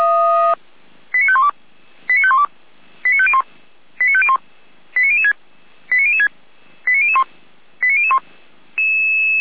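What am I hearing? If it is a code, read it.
Five-tone selective call (ZVEI), heard as quick runs of five tones, tones of about 70 ms. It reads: 89512 89512 89501 89501 89006 89006 89001 89001
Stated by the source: ZVEI five-tone call received by SDR, archived by Feuerwehr Orlamünde.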